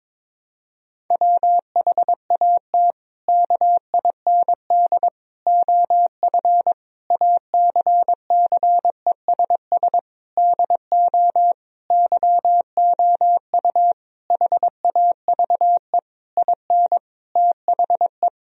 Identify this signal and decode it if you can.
Morse code, heard as a keyed tone, about 22 wpm, 700 Hz. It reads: WHAT KIND OF ACCESS DO YOU HAVE IN THE
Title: Sentences from Top 1000 Words - Review Part 4 - 22wpm